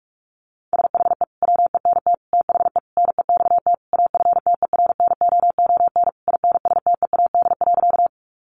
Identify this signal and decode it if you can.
Morse code, heard as a keyed tone, about 44 wpm, 700 Hz. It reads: SHE WENT THE NEXT AFTERNOON INSTEAD.
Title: Sentences from Top 1500 Words 45wpm